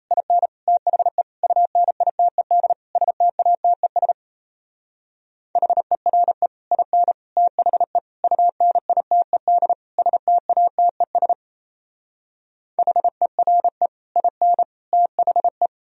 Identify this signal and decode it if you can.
Morse code, heard as a keyed tone, about 33 wpm, 700 Hz. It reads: IN THE UNITED STATES HERE IN THE UNITED STATES HERE IN THE
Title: Speed-Racing - Sets of 5 Words 28wpm